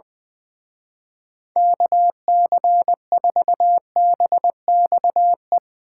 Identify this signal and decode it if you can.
Morse code, heard as a keyed tone, about 22 wpm, 700 Hz. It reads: KC4BXE
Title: Speed Racing - USA 2x3 Call Signs 20wpm